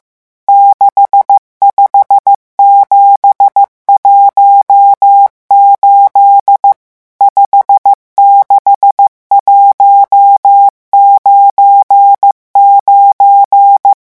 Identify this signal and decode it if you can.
Morse code, heard as a keyed tone, about 15 wpm, 800 Hz. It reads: 65718 56199
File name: morse mp3 / revision11 chiffres vitesse 14